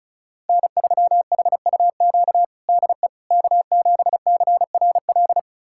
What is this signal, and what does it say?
Morse code, heard as a keyed tone, about 36 wpm, 700 Hz. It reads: N3HUQ DE K7CRL
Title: USA Callsign DE Callsign - 5 Characters - 35wpm